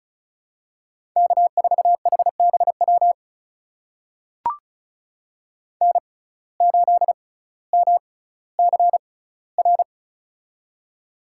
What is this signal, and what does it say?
Morse code, heard as a keyed tone, about 36 wpm, spaced as for 18 wpm, 700 Hz. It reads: K4HBW N8MCR
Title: USA Call Signs - 5 Characters - 6x Character Spacing - 35wpm